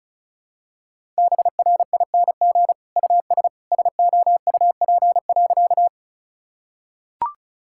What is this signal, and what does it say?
Morse code, heard as a keyed tone, about 36 wpm, 700 Hz. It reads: BRING US SOUP.
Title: Sentences from Top 1000 Words - Review Part 6 - 35wpm